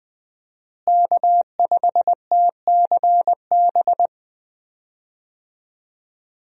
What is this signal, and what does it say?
Morse code, heard as a keyed tone, about 20 wpm, 700 Hz. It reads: K5TCB
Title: USA Call Signs - 5 Characters - Repeated 2x - 20wpm